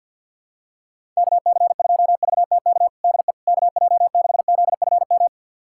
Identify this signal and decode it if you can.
Morse code, heard as a keyed tone, about 51 wpm, 700 Hz. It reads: KC2VTK DE KJ6ZFM
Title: USA Callsign DE Callsign 50wpm